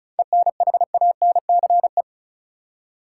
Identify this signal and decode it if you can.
Morse code, heard as a keyed tone, about 36 wpm, 700 Hz. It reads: ENHANCE
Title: Seven Letter Words - 35wpm